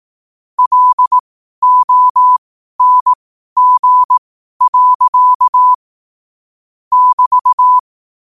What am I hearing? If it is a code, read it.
Morse code, heard as a keyed tone, about 18 wpm, spaced as for 12 wpm, 1000 Hz. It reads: LONG. <BT>
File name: qotdfiles / 12wpm